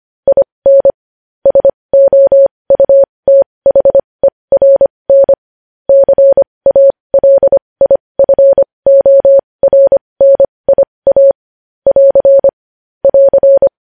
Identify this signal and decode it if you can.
Morse code, heard as a keyed tone, about 25 wpm, spaced as for 20 wpm, 550 Hz. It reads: IN SOUTHERN CALIFORNIA <AR> <AR>
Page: news headlines in morse code at 20 wpm